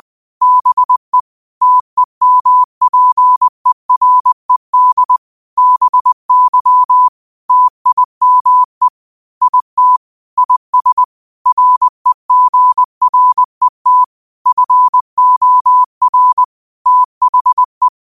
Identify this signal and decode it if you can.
Morse code, heard as a keyed tone, about 20 wpm, 1000 Hz. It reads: BE TEMPERED BY TIME IT IS REGRET FOR THE